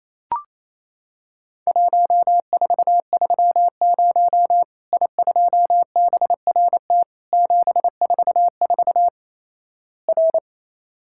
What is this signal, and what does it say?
Morse code, heard as a keyed tone, about 28 wpm, 700 Hz. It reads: 1430 I2BRT 744 R
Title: Worked All Europe DX Contest - QTC Exchange - 28wpm